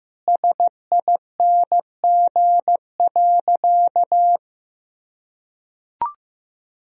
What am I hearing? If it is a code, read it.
Morse code, heard as a keyed tone, about 15 wpm, 700 Hz. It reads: SING.
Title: Sentences from Top 1000 Words - Review Part 2 - 15wpm